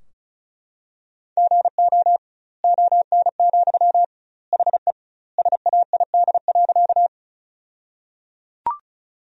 Morse code, heard as a keyed tone, about 36 wpm, 700 Hz. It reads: GO ON, HE SAID.
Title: 2x Word-Spacing - Sentences from Top 100 Words - 35wpm